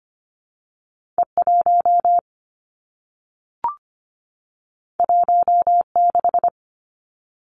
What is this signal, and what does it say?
Morse code, heard as a keyed tone, about 25 wpm, 700 Hz. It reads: E1 16
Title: Two Challenging Letter-Number Combinations 25wpm